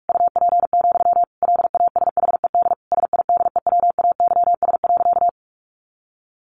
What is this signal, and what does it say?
Morse code, heard as a keyed tone, about 44 wpm, 700 Hz. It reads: UP, LASHED SIDEWAYS.
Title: A Sound Of Thunder 45wpm